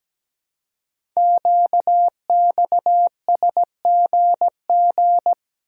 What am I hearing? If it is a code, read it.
Morse code, heard as a keyed tone, about 17 wpm, 700 Hz. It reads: QXSGG
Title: Code Groups - 5 Letter:Numbers - 17wpm